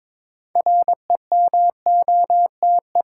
Morse code, heard as a keyed tone, about 22 wpm, 700 Hz. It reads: REMOTE
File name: Difficult Words with Hints - 22wpm